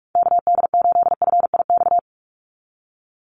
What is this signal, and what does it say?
Morse code, heard as a keyed tone, about 44 wpm, 700 Hz. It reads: KD8FIX
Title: USA Call Signs 45wpm